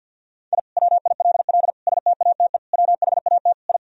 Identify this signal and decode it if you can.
Morse code, heard as a keyed tone, about 47 wpm, 700 Hz. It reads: I WILL STATE WHAT I
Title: Sentences from Top 1000 Words - Review Part 2 - 50wpm